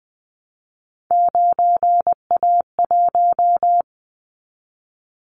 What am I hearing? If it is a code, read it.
Morse code, heard as a keyed tone, about 19 wpm, 700 Hz. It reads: 9A1
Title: Callsign Prefixes for Top 25 DXCC - 20wpm